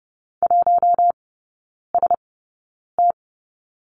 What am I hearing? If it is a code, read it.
Morse code, heard as a keyed tone, about 30 wpm, spaced as for 9 wpm, 700 Hz. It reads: 1ST
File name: course / 30:10wpm / 087 Lesson 15e - Number 5 - Three Characters 30wpm